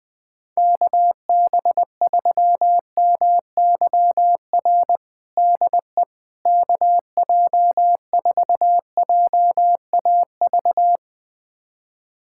Morse code, heard as a keyed tone, about 20 wpm, 700 Hz. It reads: KB3MYR DE KJ4JAV
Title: USA Callsign DE Callsign - 6 Characters - 20wpm